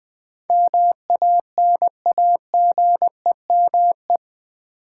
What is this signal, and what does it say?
Morse code, heard as a keyed tone, about 20 wpm, 700 Hz. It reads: MANAGEME
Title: Build A Word - 4 Letters At A Time - 10-Letter Words - 20wpm